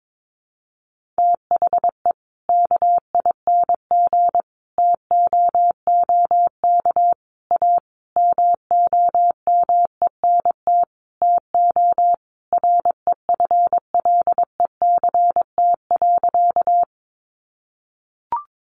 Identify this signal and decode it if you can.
Morse code, heard as a keyed tone, about 22 wpm, 700 Hz. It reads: THE KING TOOK A MOMENT TO REFLECT.
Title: Sentences from Top 1500 Words 22wpm